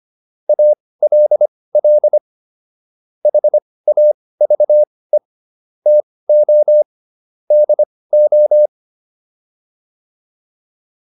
Morse code, heard as a keyed tone, about 25 wpm, spaced as for 18 wpm, 600 Hz. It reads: ALL HAVE TO DO